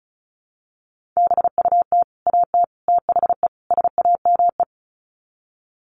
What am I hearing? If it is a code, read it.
Morse code, heard as a keyed tone, about 34 wpm, 700 Hz. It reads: BUT AT THE SAME
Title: Sets of 4 Words 35wpm